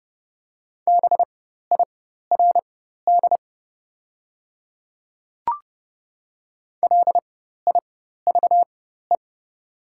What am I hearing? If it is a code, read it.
Morse code, heard as a keyed tone, about 30 wpm, spaced as for 14 wpm, 700 Hz. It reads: BIRD LIVE
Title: 4x Character Spacing - Top 500 Words - 30wpm